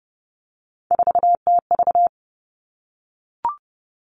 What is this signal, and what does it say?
Morse code, heard as a keyed tone, about 30 wpm, 700 Hz. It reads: 4TV
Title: Three Challenging Letter-Number Combinations 30wpm